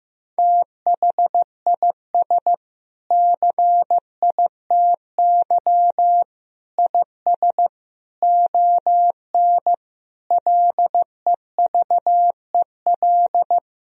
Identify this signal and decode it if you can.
Morse code, heard as a keyed tone, about 15 wpm, 700 Hz. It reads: THIS CITY IS ON LEVEL
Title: Sentences from Top 1000 Words - Review Part 4 - 15wpm